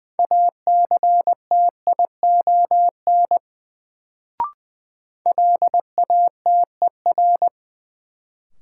Morse code, heard as a keyed tone, about 20 wpm, 700 Hz. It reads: ACTION LATER
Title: Sentences from Top 700 Words - New Word Drill - 20wpm